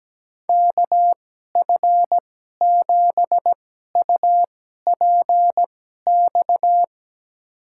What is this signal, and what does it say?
Morse code, heard as a keyed tone, about 17 wpm, spaced as for 12 wpm, 700 Hz. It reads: KF7UPX
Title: USA Call Signs - 6 Characters - 2x Character Spacing - 17wpm